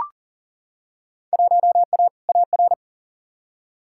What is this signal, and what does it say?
Morse code, heard as a keyed tone, about 40 wpm, 700 Hz. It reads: 1A AR
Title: Field Day Exchanges Encore - 40wpm